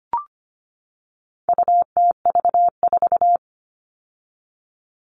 Morse code, heard as a keyed tone, about 25 wpm, 700 Hz. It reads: UTV4
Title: All Cut Numbers - 4-Digit 25wpm